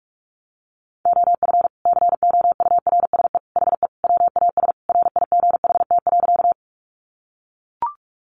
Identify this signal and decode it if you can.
Morse code, heard as a keyed tone, about 44 wpm, 700 Hz. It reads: OF COURSE HE WAS RIGHT.
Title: Sentences from Top 500 Words 45wpm